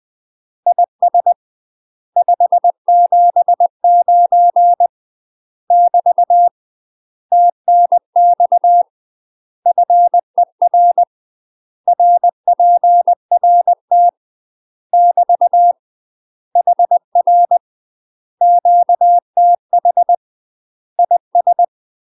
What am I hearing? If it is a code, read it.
Morse code, heard as a keyed tone, about 20 wpm, 700 Hz. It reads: IS 579 <BT> TNX FER RPRT <BT> HR QTH IS
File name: MASTER QSOs TRAINING/20WPM